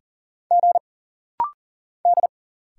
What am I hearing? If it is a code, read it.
Morse code, heard as a keyed tone, about 40 wpm, 700 Hz. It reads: G D
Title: Single Letters - Rapid-Fire 40wpm